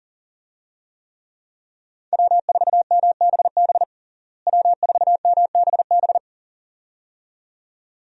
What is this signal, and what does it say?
Morse code, heard as a keyed tone, about 40 wpm, 700 Hz. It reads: W4MBB W4MBB
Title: USA Call Signs - 5 Characters - Repeated 2x - 40wpm